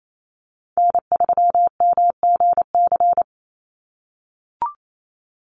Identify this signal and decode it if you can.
Morse code, heard as a keyed tone, about 28 wpm, 700 Hz. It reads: N3MGC
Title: USA Call Signs 28wpm